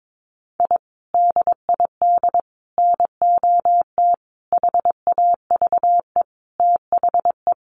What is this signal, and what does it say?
Morse code, heard as a keyed tone, about 22 wpm, 700 Hz. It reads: I DID NOT HAVE THE
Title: Sets of 5 Words Encore 22wpm